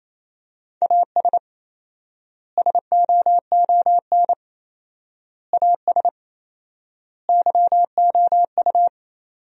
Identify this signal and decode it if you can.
Morse code, heard as a keyed tone, about 27 wpm, 700 Hz. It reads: AS SOON AS YOU
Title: 4x Word-Spacing - Sentences from Top 400 Words - 28wpm